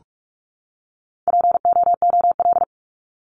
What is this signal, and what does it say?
Morse code, heard as a keyed tone, about 44 wpm, 700 Hz. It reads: POOL
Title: Top 1751-2000 Words 45wpm